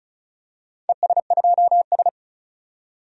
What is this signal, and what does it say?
Morse code, heard as a keyed tone, about 34 wpm, 700 Hz. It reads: ES2S
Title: DXCC-100 Call Signs - 4 Characters - 35wpm